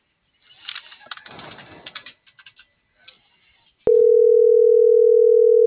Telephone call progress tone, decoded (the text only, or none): ringback tone